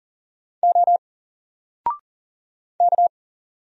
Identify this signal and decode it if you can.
Morse code, heard as a keyed tone, about 40 wpm, 700 Hz. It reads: O K
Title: Single Letter-Number 40wpm